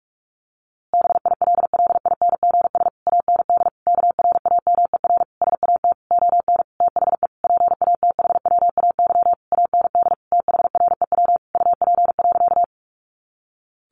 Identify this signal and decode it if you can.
Morse code, heard as a keyed tone, about 44 wpm, 700 Hz. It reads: BILLINGS AND KRAMER SAT ON THE PATHWAY AND THREW UP.
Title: A Sound Of Thunder 45wpm